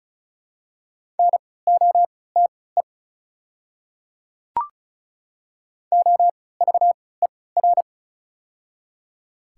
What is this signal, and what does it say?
Morse code, heard as a keyed tone, about 34 wpm, spaced as for 20 wpm, 700 Hz. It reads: NOTE OVER
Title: 3x Character Spacing - Top 500 Words - 35wpm